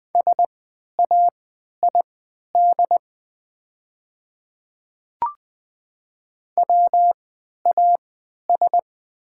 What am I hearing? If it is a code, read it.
Morse code, heard as a keyed tone, about 20 wpm, spaced as for 11 wpm, 690 Hz. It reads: SAID WAS
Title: 3x Character Spacing - Top 100 Words - 20wpm